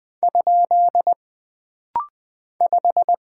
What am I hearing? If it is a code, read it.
Morse code, heard as a keyed tone, about 20 wpm, 700 Hz. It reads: ? 5
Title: Single Character (Letter:Number:Punctuation:Prosign) - Rapid-Fire - 20wpm